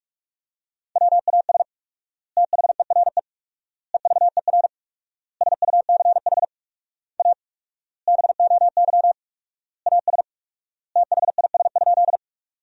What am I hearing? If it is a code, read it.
Morse code, heard as a keyed tone, about 46 wpm, 700 Hz. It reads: WAS THERE EVER SUCH A BOY AS THIS?
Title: 4x Word-Spacing - Sentences from Top 400 Words - 45wpm